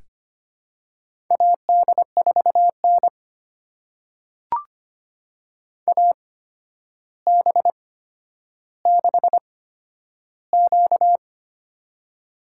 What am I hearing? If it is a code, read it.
Morse code, heard as a keyed tone, about 25 wpm, spaced as for 11 wpm, 700 Hz. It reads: AD4N AB6Q
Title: USA Call Signs - 4 Characters - 8x Character Spacing - 25wpm